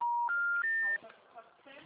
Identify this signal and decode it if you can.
Special information tone, with speech.
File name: out- -2007- - - .2895